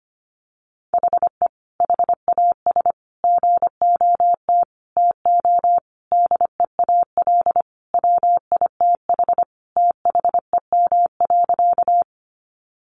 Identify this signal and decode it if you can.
Morse code, heard as a keyed tone, about 25 wpm, 700 Hz. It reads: HE HAS GOT TO DEAL WITH THEM.
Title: Sentences from Top 800 Words 25wpm